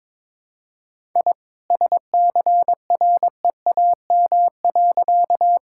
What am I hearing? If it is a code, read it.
Morse code, heard as a keyed tone, about 22 wpm, 700 Hz. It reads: I SCREAM.
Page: morse-code-ninja-practice-22wpm